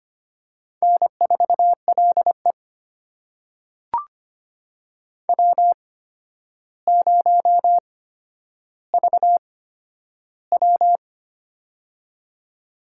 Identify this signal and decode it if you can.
Morse code, heard as a keyed tone, about 25 wpm, spaced as for 11 wpm, 700 Hz. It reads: N4LE W0VW
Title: USA Call Signs - 4 Characters - 8x Character Spacing - 25wpm